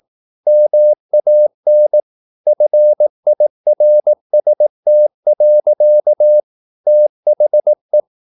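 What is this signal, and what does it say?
Morse code, heard as a keyed tone, about 18 wpm, 600 Hz. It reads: MAN FIRST. THE
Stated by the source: Morse practice at 18 wpm, English text, from CW med Gnister.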